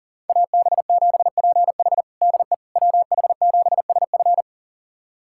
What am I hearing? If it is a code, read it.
Morse code, heard as a keyed tone, about 37 wpm, 700 Hz. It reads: AB7PH DE WH7SF
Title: USA Callsign DE Callsign - 5 Characters - 40wpm